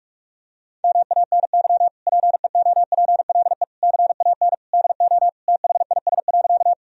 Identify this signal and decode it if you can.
Morse code, heard as a keyed tone, about 44 wpm, 700 Hz. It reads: MANY PEOPLE CAN DO THIS.
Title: Sentences from Top 1000 Words - Review Part 1 - 45wpm